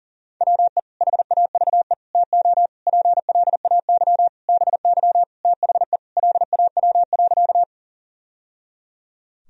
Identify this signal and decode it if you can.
Morse code, heard as a keyed tone, about 37 wpm, 700 Hz. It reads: WE HAVE TO PLAY BY THE LAW.
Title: Sentences from Top 1000 Words - Review Part 3 - 40wpm